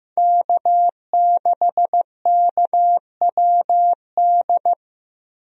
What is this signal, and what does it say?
Morse code, heard as a keyed tone, about 15 wpm, 700 Hz. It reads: K6KWD
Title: USA Call Signs - 5 Characters - Repeated 3x - 15wpm